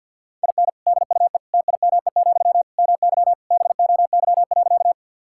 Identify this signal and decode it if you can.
Morse code, heard as a keyed tone, about 52 wpm, 700 Hz. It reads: IN DUE TIME, MY BOY.
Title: Sentences from Top 1000 Words - Review Part 6 - 50wpm